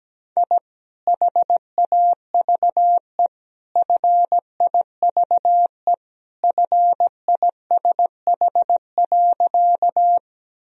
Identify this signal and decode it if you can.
Morse code, heard as a keyed tone, about 17 wpm, 700 Hz. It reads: I HAVE FIVE FISH.
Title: Sentences from Top 1000 Words - Review Part 2 - 17wpm